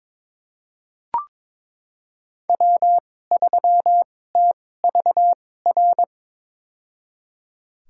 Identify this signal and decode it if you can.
Morse code, heard as a keyed tone, about 22 wpm, spaced as for 16 wpm, 700 Hz. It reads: W3TVR
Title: USA Call Signs - 5 Characters - 2x Character Spacing - 22wpm